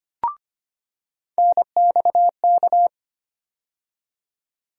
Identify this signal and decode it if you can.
Morse code, heard as a keyed tone, about 25 wpm, 700 Hz. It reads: NXK
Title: USA Call Sign Postfixes - 3 Letters 25wpm